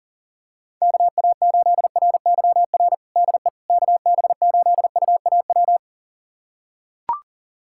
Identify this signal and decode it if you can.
Morse code, heard as a keyed tone, about 37 wpm, 700 Hz. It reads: KA8RYR DE KB8UAW
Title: USA Callsign DE Callsign 40wpm